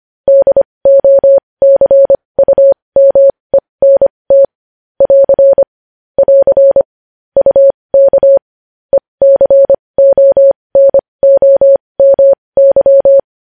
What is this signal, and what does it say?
Morse code, heard as a keyed tone, about 25 wpm, spaced as for 20 wpm, 550 Hz. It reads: DOCUMENT <AR> <AR> UK ECONOMY
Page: news headlines in morse code at 20 wpm